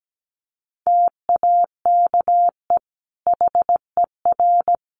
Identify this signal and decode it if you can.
Morse code, heard as a keyed tone, about 17 wpm, 700 Hz. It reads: TAKE HER
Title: Sentences from Top 1100 Words 17wpm